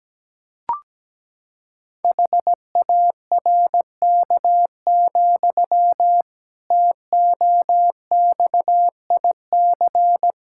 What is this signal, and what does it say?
Morse code, heard as a keyed tone, about 17 wpm, 700 Hz. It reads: HARK, TOXIC
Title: Pangram Challenge v2 - 17wpm